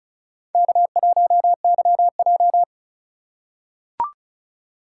Morse code, heard as a keyed tone, about 36 wpm, 700 Hz. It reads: K1YJ
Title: USA Call Signs - 4 Characters - 4x Character Spacing - 35wpm